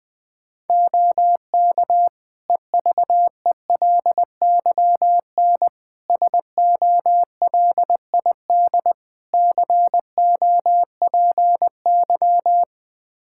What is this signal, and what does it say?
Morse code, heard as a keyed tone, about 19 wpm, 700 Hz. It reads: OK EVELYN SOLID COPY
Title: Practice QSO - Exchange 3 - More Info - 20wpm